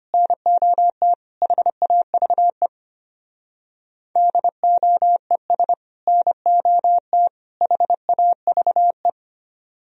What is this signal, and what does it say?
Morse code, heard as a keyed tone, about 30 wpm, 700 Hz. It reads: NOT HAVE DOES NOT HAVE
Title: Speed-Racing - Sets of 3 Words 25wpm